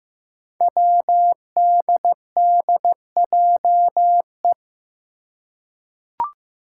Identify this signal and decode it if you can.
Morse code, heard as a keyed tone, about 15 wpm, 700 Hz. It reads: WDDJE
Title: Five Challenging Letter-Number Combinations 15wpm